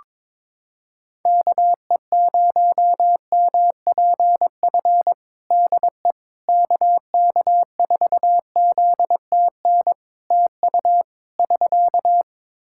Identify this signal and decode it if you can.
Morse code, heard as a keyed tone, about 22 wpm, 700 Hz. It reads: KE0MPF DE KK4ZTN TU <SK>
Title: Practice QSO - Exchange 4 - Ending - 22wpm